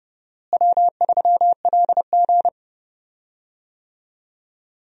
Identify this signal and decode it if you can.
Morse code, heard as a keyed tone, about 31 wpm, 700 Hz. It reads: W3LG
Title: USA Call Signs - 4 Characters - Repeated 3x - 30wpm